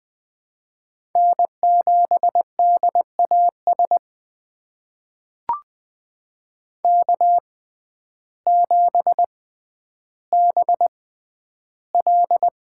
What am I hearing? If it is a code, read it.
Morse code, heard as a keyed tone, about 20 wpm, spaced as for 11 wpm, 700 Hz. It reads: N7DAS K7BL
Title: USA Call Signs - 5 Characters - 6x Character Spacing - 20wpm